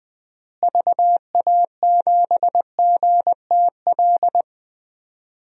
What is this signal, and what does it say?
Morse code, heard as a keyed tone, about 20 wpm, 700 Hz. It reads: VA7GTL
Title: DXCC-100 Call Signs - 6 Characters - Repeated 3x - 20wpm